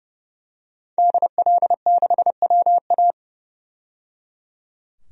Morse code, heard as a keyed tone, about 29 wpm, 700 Hz. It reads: DL6WA
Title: DXCC-100 Call Signs - 5 Characters - 30wpm